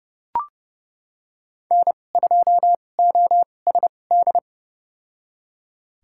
Morse code, heard as a keyed tone, about 32 wpm, spaced as for 22 wpm, 700 Hz. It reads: N2OSD
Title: USA Call Signs - 2x Character Spacing - 30wpm